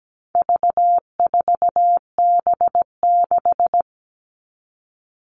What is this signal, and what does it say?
Morse code, heard as a keyed tone, about 17 wpm, 700 Hz. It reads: V4B6
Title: All Cut Numbers - 4-Digit 17wpm